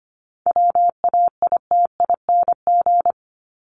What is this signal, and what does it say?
Morse code, heard as a keyed tone, about 25 wpm, 700 Hz. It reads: WAITING
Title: ING Suffix - Top 100 Words with Base Words 25wpm